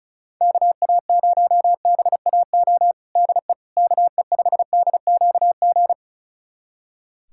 Morse code, heard as a keyed tone, about 34 wpm, 700 Hz. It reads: KA0BAO DE KE5DQG
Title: USA Callsign DE Callsign 35wpm